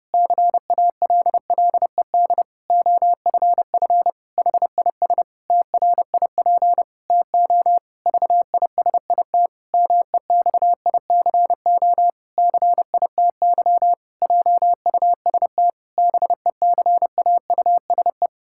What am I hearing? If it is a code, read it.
Morse code, heard as a keyed tone, about 28 wpm, 700 Hz. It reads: CALLED OFF HIS TRIP TO VISIT MEXICO CITY JUST BECAUSE
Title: Pangram Challenge v2 - 30wpm